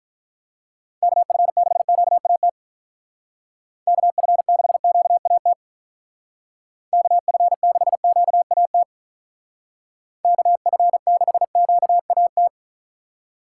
Morse code, heard as a keyed tone, about 42 wpm, 700 Hz. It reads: KF6QAT KF6QAT KF6QAT KF6QAT
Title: Speed Racing - USA 2x3 Call Signs 35wpm